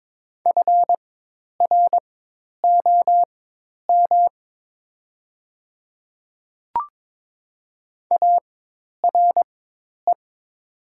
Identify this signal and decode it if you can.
Morse code, heard as a keyed tone, about 22 wpm, spaced as for 10 wpm, 700 Hz. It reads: FROM ARE
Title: 4x Character Spacing - Top 100 Words - 22wpm